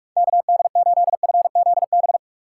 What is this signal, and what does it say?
Morse code, heard as a keyed tone, about 42 wpm, 700 Hz. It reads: KD8FZB
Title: USA Call Signs - 6x Character Spacing - 45wpm